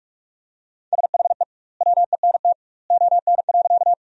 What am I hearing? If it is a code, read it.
Morse code, heard as a keyed tone, about 42 wpm, 700 Hz. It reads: SHE WENT ON.
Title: 2x Word-Spacing - Sentences from Top 200 Words - 45wpm